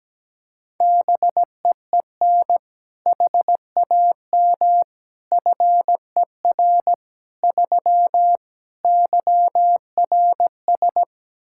Morse code, heard as a keyed tone, about 17 wpm, 700 Hz. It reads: BEEN HAM FER 3 YRS
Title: K1USN SST QSO Practice 17wpm